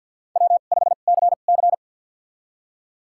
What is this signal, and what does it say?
Morse code, heard as a keyed tone, about 52 wpm, spaced as for 37 wpm, 700 Hz. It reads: W5CC
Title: USA Call Signs - 4 Characters - 2x Character Spacing - 50wpm